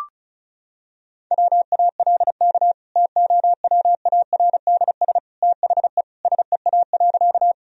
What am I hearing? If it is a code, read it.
Morse code, heard as a keyed tone, about 36 wpm, 700 Hz. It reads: WALK TOWARDS THE SEA.
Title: Sentences from Top 1000 Words - Review Part 2 - 35wpm